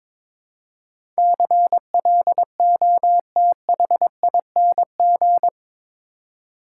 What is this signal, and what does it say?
Morse code, heard as a keyed tone, about 22 wpm, 700 Hz. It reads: CLOTHING
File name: practice / 22wpm / Eight Letter Words - 22wpm